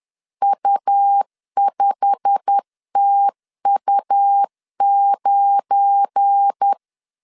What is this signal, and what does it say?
Morse code, heard as a keyed tone, about 10 wpm, 800 Hz. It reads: U5TU9
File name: CW / DET / 1 9 1500C S B 10wpm 4439